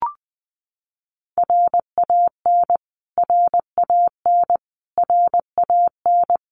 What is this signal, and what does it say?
Morse code, heard as a keyed tone, about 20 wpm, 700 Hz. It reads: RAN RAN RAN
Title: Top 500 Words Repeated 20wpm